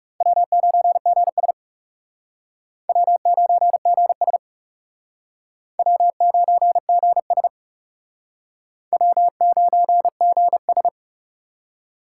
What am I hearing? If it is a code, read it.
Morse code, heard as a keyed tone, about 37 wpm, 700 Hz. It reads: W9GS W9GS W9GS W9GS
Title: Speed Racing - USA 1x2 Call Signs 30wpm